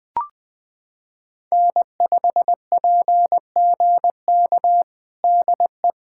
Morse code, heard as a keyed tone, about 21 wpm, 700 Hz. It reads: N5PGK DE
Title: USA Callsign DE Callsign - 5 Characters - 20wpm